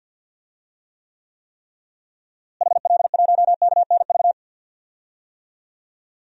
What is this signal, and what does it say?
Morse code, heard as a keyed tone, about 46 wpm, 700 Hz. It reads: HL1KNV
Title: DXCC-100 Call Signs - 6 Characters - Repeated 3x - 50wpm